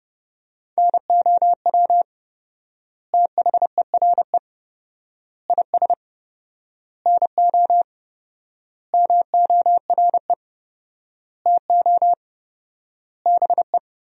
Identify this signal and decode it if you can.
Morse code, heard as a keyed tone, about 30 wpm, 700 Hz. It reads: NOW THERE IS NO MORE TO BE
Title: 4x Word-Spacing - Sentences from Top 100 Words - 30wpm